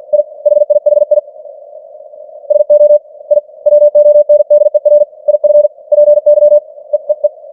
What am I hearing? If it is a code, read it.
Morse code, heard as a keyed tone, about 46 wpm, 590 Hz. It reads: I HIHI SO I WONDER IF PY EEE
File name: archived / CWBros